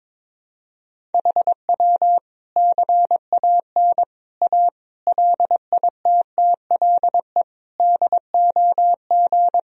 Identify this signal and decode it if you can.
Morse code, heard as a keyed tone, about 22 wpm, 700 Hz. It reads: HW CAN A LITTLE DOG
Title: Sentences from Top 500 Words and Abbreviations 22wpm